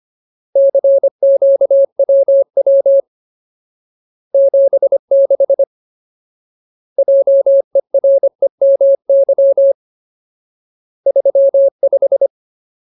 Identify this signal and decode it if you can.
Morse code, heard as a keyed tone, about 25 wpm, 550 Hz. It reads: CQWW 76 JEREMY 35
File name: Ultimate Word List / 25WPM Ultimate